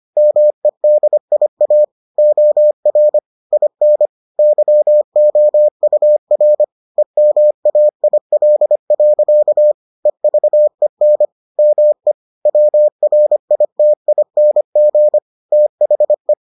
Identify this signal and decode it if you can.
Morse code, heard as a keyed tone, about 25 wpm, 600 Hz. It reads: MEDIA OR IN YOUR EMAIL. EVEN ME WRITING THE